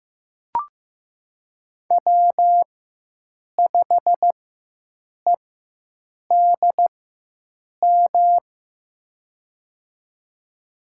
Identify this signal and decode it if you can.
Morse code, heard as a keyed tone, about 15 wpm, spaced as for 7 wpm, 700 Hz. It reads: W5EDM
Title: USA Call Signs - 5 Characters - 4x Character Spacing - 15wpm